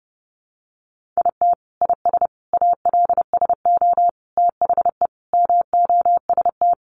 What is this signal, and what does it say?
Morse code, heard as a keyed tone, about 30 wpm, 700 Hz. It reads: IT IS ALSO THE MOST
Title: Sentences from Top 1400 Words 30wpm